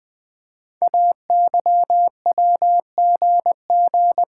Morse code, heard as a keyed tone, about 20 wpm, 700 Hz. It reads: AYWGG
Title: Code Groups - 5 Letters - 20wpm